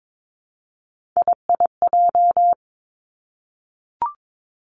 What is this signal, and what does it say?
Morse code, heard as a keyed tone, about 22 wpm, 700 Hz. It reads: IIJ
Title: Three Challenging Letter-Number Combinations 22wpm